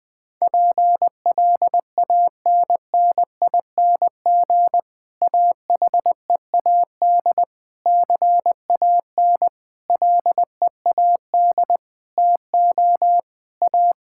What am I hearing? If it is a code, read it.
Morse code, heard as a keyed tone, about 19 wpm, 700 Hz. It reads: PLANNING AHEAD CAN LEAD TO A
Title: Difficult Sentences with Hints - 20wpm